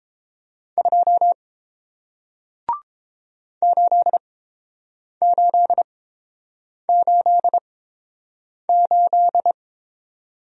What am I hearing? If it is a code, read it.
Morse code, heard as a keyed tone, about 30 wpm, spaced as for 7 wpm, 700 Hz. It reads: 2 8888